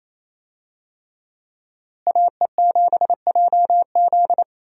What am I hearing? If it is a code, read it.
Morse code, heard as a keyed tone, about 28 wpm, 700 Hz. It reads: AE7JZ
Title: USA Call Signs - 5 Characters - Repeated 3x - 28wpm